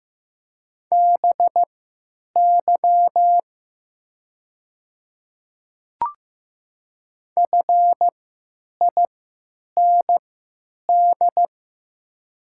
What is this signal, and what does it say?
Morse code, heard as a keyed tone, about 15 wpm, spaced as for 9 wpm, 700 Hz. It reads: BY FIND
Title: 3x Character Spacing - Top 300 Words - 15wpm